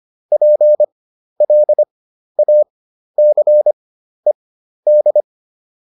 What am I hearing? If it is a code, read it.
Morse code, heard as a keyed tone, about 25 wpm, spaced as for 12 wpm, 600 Hz. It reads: PLACED
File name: ED6 12